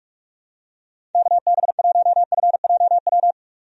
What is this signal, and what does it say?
Morse code, heard as a keyed tone, about 46 wpm, 700 Hz. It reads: KB1FJW
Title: USA Call Signs - 6 Characters - 45wpm